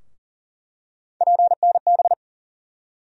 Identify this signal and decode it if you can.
Morse code, heard as a keyed tone, about 40 wpm, 700 Hz. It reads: PNB
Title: USA Call Sign Postfixes - 3 Letters 40wpm